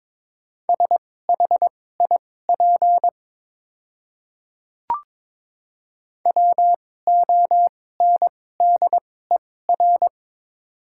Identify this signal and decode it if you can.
Morse code, heard as a keyed tone, about 22 wpm, spaced as for 16 wpm, 700 Hz. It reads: SHIP WONDER
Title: 2x Character Spacing - Top 500 Words - 22wpm